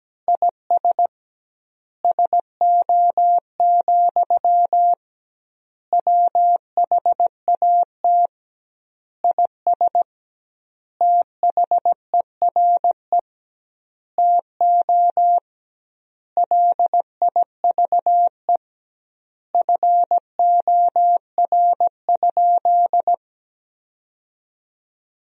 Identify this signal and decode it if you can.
Morse code, heard as a keyed tone, about 17 wpm, 700 Hz. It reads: IS SO, WHAT IS THERE TO LIVE FOR?
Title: 2x Word-Spacing - Sentences from Top 200 Words - 17wpm